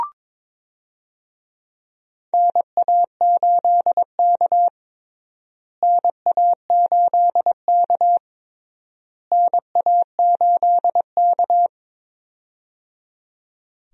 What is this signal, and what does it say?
Morse code, heard as a keyed tone, about 22 wpm, 700 Hz. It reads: NA8K NA8K NA8K
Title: USA Call Signs - 4 Characters - Repeated 3x - 22wpm